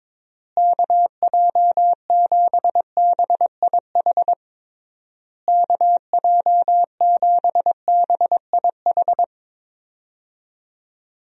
Morse code, heard as a keyed tone, about 22 wpm, 700 Hz. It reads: KJ7BIH KJ7BIH
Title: USA Call Signs - 6 Characters - Repeated 3x - 22wpm